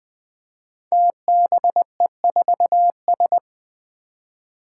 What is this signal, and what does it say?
Morse code, heard as a keyed tone, about 20 wpm, 700 Hz. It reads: TBE4S
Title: Five Challenging Letter-Number Combinations 20wpm